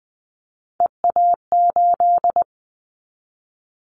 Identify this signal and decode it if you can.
Morse code, heard as a keyed tone, about 20 wpm, 700 Hz. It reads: EA8
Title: Callsign Prefixes for Top 100 DXCC - 20wpm